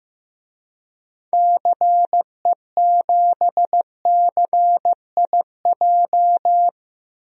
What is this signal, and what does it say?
Morse code, heard as a keyed tone, about 15 wpm, 700 Hz. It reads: CE7CIJ
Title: DXCC-100 Call Signs - 6 Characters - 15wpm